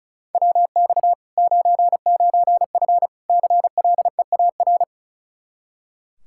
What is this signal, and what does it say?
Morse code, heard as a keyed tone, about 36 wpm, 700 Hz. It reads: WX 99F CLEAR
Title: Practice QSO - Exchange 2 - Info - 35wpm